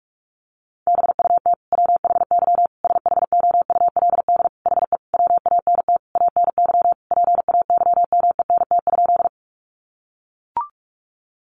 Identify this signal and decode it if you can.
Morse code, heard as a keyed tone, about 44 wpm, 700 Hz. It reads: BUT WHY SHOULD HE WANT ANY PAYMENT?
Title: Sentences from Top 1200 Words 45wpm